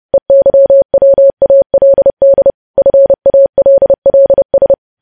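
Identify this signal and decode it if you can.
Morse code, heard as a keyed tone, about 30 wpm, 550 Hz. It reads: EYWALD FALLS
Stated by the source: Morse code practice of news headlines at 30.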